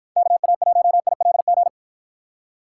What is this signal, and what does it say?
Morse code, heard as a keyed tone, about 51 wpm, 700 Hz. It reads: KA1ILG